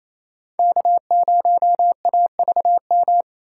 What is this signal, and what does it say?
Morse code, heard as a keyed tone, about 28 wpm, 700 Hz. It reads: K0AVM
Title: USA Call Signs - 5 Characters - 4x Character Spacing - 28wpm